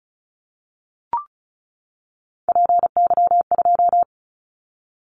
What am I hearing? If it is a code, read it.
Morse code, heard as a keyed tone, about 36 wpm, 700 Hz. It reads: PY2
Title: Callsign Prefixes for Top 50 DXCC - 35wpm